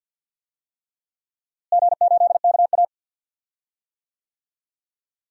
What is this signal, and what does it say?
Morse code, heard as a keyed tone, about 50 wpm, 700 Hz. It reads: G8KA
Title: DXCC-100 Call Signs - 4 Characters - Repeated 3x - 50wpm